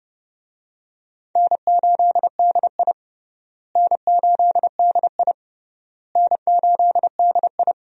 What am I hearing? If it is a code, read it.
Morse code, heard as a keyed tone, about 30 wpm, 700 Hz. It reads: N8DI N8DI N8DI
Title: USA Call Signs - 4 Characters - Repeated 3x - 30wpm